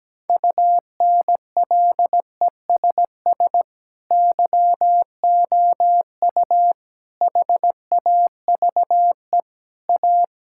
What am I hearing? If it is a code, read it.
Morse code, heard as a keyed tone, about 17 wpm, 700 Hz. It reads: UNLESS YOU HAVE A
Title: Sentences from Top 1000 Words - Review Part 5 - 17wpm